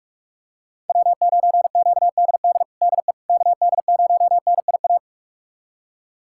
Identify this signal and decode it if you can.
Morse code, heard as a keyed tone, about 46 wpm, 700 Hz. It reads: W9QDD DE KD0NIA
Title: USA Callsign DE Callsign 45wpm